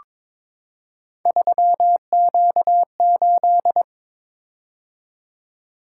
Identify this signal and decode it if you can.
Morse code, heard as a keyed tone, about 22 wpm, 700 Hz. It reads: 3Q8
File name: practice / 22wpm / Three Letter-Number Combinations - 22wpm